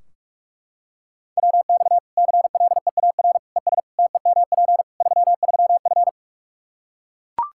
Morse code, heard as a keyed tone, about 44 wpm, 700 Hz. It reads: WX CLEAR ES TEMP 33F
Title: Practice QSO - Exchange 2 - Info - 45wpm